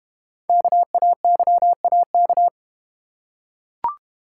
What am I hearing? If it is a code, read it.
Morse code, heard as a keyed tone, about 32 wpm, 700 Hz. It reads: KAYAK